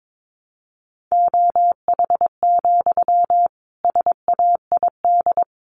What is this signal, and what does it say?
Morse code, heard as a keyed tone, about 22 wpm, 700 Hz. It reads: OH, SAID
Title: Sentences from Top 500 Words 22wpm